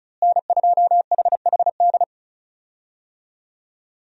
Morse code, heard as a keyed tone, about 36 wpm, 700 Hz. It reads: N2HHD
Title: USA Call Signs - 5 Characters - Repeated 2x - 35wpm